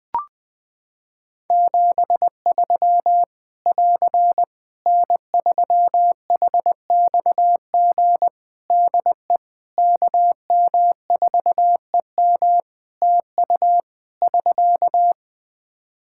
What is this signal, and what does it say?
Morse code, heard as a keyed tone, about 20 wpm, 700 Hz. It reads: 73 <AR> N3HXG DE KM4EM TU <SK>
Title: Practice QSO - Exchange 4 - Ending - 20wpm